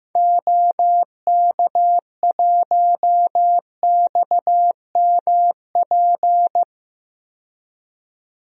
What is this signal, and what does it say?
Morse code, heard as a keyed tone, about 15 wpm, 700 Hz. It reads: OK1XMP
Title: DXCC-100 Call Signs - 6 Characters - 15wpm